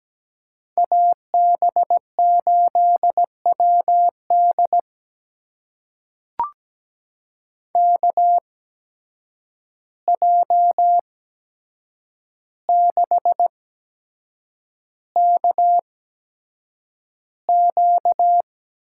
Morse code, heard as a keyed tone, about 17 wpm, spaced as for 7 wpm, 700 Hz. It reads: AB8WD KJ6KQ
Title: USA Call Signs - 8x Character Spacing - 17wpm